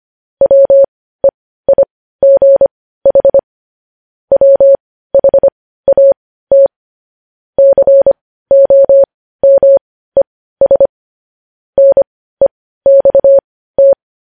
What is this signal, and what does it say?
Morse code, heard as a keyed tone, about 25 wpm, spaced as for 15 wpm, 550 Hz. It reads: WEIGH WHAT COMES NEXT
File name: podcast / News headlines-1x-25x15